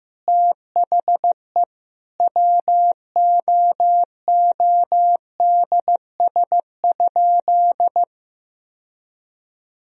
Morse code, heard as a keyed tone, about 15 wpm, 700 Hz. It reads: THE WOODS?
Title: Sentences from Top 1000 Words - Review Part 2 - 15wpm